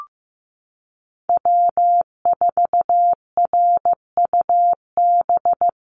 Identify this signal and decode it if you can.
Morse code, heard as a keyed tone, about 15 wpm, 700 Hz. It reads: W4RUB
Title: USA Call Signs Repeated 2x 15wpm